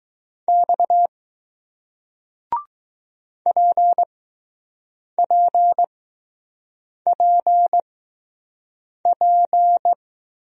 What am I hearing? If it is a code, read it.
Morse code, heard as a keyed tone, about 20 wpm, spaced as for 6 wpm, 700 Hz. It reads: X PPPP